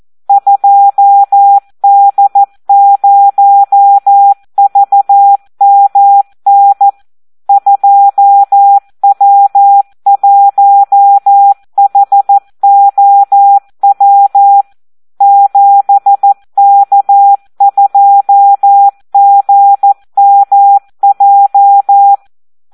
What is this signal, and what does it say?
Morse code, heard as a keyed tone, about 14 wpm, 800 Hz. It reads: 2D0VMN 2W1HOW 7K2GMJ
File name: Audio practica CW / SD 1 14